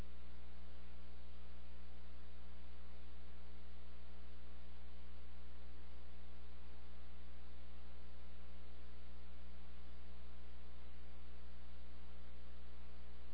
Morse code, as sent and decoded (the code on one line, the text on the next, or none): none